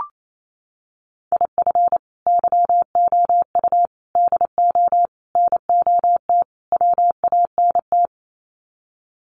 Morse code, heard as a keyed tone, about 28 wpm, 700 Hz. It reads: IF YOU DO NOT WANT
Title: Sets of 5 Words 28wpm